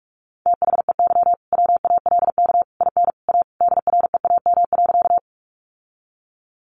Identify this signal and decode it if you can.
Morse code, heard as a keyed tone, about 44 wpm, 700 Hz. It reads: THEY WALK IN A DREAM.
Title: Sentences from Top 1500 Words 45wpm